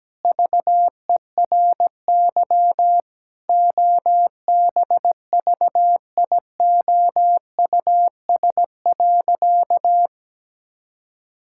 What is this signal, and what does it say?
Morse code, heard as a keyed tone, about 17 wpm, 700 Hz. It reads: VERY OBVIOUS.
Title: Sentences from Top 1600 Words 17wpm